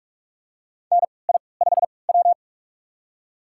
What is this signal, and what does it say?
Morse code, heard as a keyed tone, about 46 wpm, spaced as for 26 wpm, 700 Hz. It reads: NI5W
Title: USA Call Signs - 3x Character Spacing - 45wpm